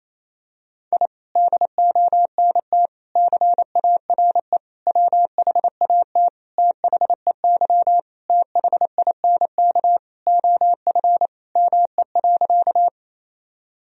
Morse code, heard as a keyed tone, about 27 wpm, 700 Hz. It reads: I DONT CARE WHAT THEY THINK OF ME.
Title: Sentences from Top 1000 Words - Review Part 1 - 28wpm